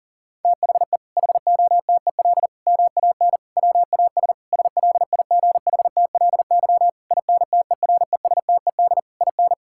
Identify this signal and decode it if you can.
Morse code, heard as a keyed tone, about 37 wpm, 700 Hz. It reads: THE HOTEL MAN WAS SLIGHTLY INTERESTED IN
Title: Sentences from Top 1600 Words 40wpm